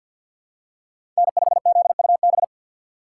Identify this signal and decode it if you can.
Morse code, heard as a keyed tone, about 53 wpm, 700 Hz. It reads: N5ZUB
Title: USA Call Signs - 5 Characters - 2x Character Spacing - 50wpm